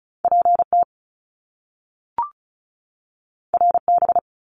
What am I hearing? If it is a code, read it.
Morse code, heard as a keyed tone, about 36 wpm, 700 Hz. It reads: PT RB
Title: Top Two Letter Combinations in Words 35wpm